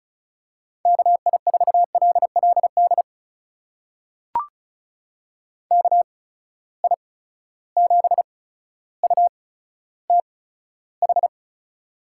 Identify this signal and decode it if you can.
Morse code, heard as a keyed tone, about 36 wpm, spaced as for 15 wpm, 700 Hz. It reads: KI4LLD KI7UTH
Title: USA Call Signs - 6 Characters - 8x Character Spacing - 35wpm